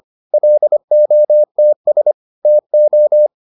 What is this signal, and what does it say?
Morse code, heard as a keyed tone, about 25 wpm, 600 Hz. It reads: LOTS TO